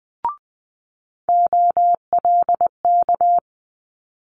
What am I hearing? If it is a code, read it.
Morse code, heard as a keyed tone, about 19 wpm, 700 Hz. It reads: OLK
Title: Multi-letter Phonemes Quick Fire 20wpm